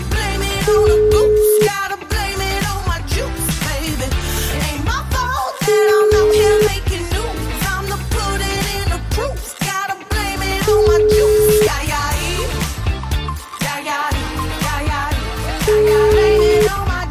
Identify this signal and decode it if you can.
Ringback tone (Europe, 425 Hz), with noise.